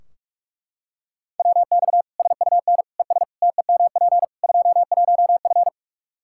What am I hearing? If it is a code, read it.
Morse code, heard as a keyed tone, about 42 wpm, 700 Hz. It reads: WX SUN ES TEMP 21F
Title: Practice QSO - Exchange 2 - Info - 45wpm